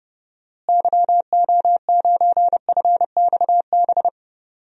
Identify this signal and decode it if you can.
Morse code, heard as a keyed tone, about 31 wpm, 700 Hz. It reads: YO9FXB
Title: DXCC-100 Call Signs - 6 Characters - Repeated 3x - 30wpm